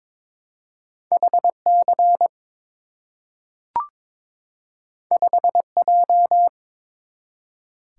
Morse code, heard as a keyed tone, about 22 wpm, 700 Hz. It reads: HC 5J
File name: practice / 22wpm / Two Letter-Number Combinations - 22wpm